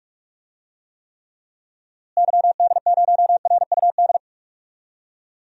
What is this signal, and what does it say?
Morse code, heard as a keyed tone, about 44 wpm, 690 Hz. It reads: YD0RUD
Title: DXCC-100 Call Signs - 6 Characters - Repeated 2x - 45wpm